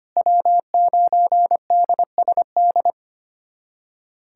Morse code, heard as a keyed tone, about 25 wpm, 700 Hz. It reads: W9DSD
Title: USA Callsign DE Callsign - 5 Characters - 25wpm